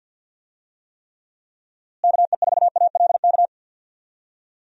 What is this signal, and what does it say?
Morse code, heard as a keyed tone, about 46 wpm, 700 Hz. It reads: KE4ALK
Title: USA Call Signs - 6 Characters - Repeated 3x - 50wpm